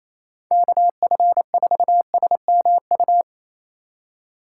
Morse code, heard as a keyed tone, about 35 wpm, 700 Hz. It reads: KF4SMU